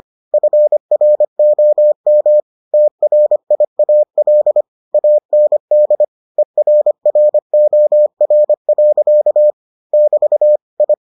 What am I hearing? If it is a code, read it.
Morse code, heard as a keyed tone, about 25 wpm, 600 Hz. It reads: FROM TRIAL AND ERROR. <BT> I